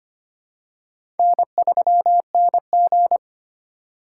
Morse code, heard as a keyed tone, about 25 wpm, 700 Hz. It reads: N3NG
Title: USA Call Signs - 2x Character Spacing - 25wpm